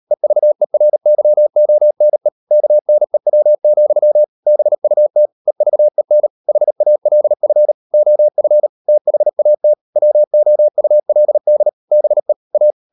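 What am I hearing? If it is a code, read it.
Morse code, heard as a keyed tone, about 38 wpm, 600 Hz. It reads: EVERYONE KNEW, BUT EVEN HALF OF THAT WOULD BE A
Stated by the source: Morse code practice at 38 WPM, sending Ken Follett's novel Never in English, from CW med Gnister.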